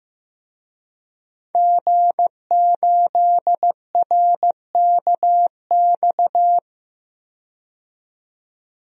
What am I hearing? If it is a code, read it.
Morse code, heard as a keyed tone, about 15 wpm, 700 Hz. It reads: G8RKX
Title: DXCC-100 Call Signs - 5 Characters - 15wpm